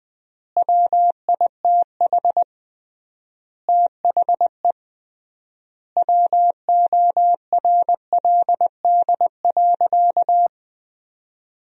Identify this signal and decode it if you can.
Morse code, heard as a keyed tone, about 21 wpm, 700 Hz. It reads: WITH THE WORLD.
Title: 3x Word-Spacing - Sentences from Top 200 Words - 20wpm